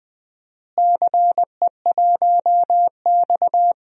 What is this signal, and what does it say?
Morse code, heard as a keyed tone, about 20 wpm, 700 Hz. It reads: CE1X
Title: DXCC-100 Call Signs - 4 Characters - Repeated 2x - 20wpm